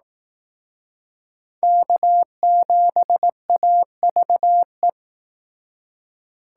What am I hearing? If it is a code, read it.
Morse code, heard as a keyed tone, about 18 wpm, 700 Hz. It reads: K7AVE
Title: Speed Racing - USA 1x3 Call Signs 15wpm